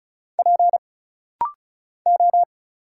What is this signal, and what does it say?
Morse code, heard as a keyed tone, about 35 wpm, 700 Hz. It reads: P O
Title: Single Letters - Rapid-Fire 35wpm